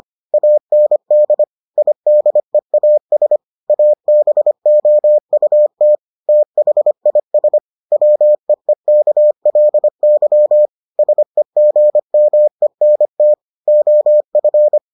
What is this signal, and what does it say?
Morse code, heard as a keyed tone, about 25 wpm, 600 Hz. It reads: AND IDEAS ABOUT THIS WEEKLY SEGMENT OF